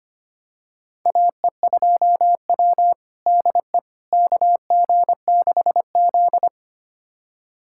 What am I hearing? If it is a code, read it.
Morse code, heard as a keyed tone, about 25 wpm, 700 Hz. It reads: AE2W DE KG6Z
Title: USA Callsign DE Callsign - 4 Characters - 25wpm